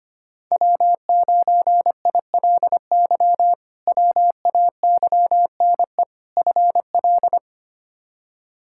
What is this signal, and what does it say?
Morse code, heard as a keyed tone, about 25 wpm, 700 Hz. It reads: W9ILY WAYNE FL
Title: K1USN SST QSO Practice - 25wpm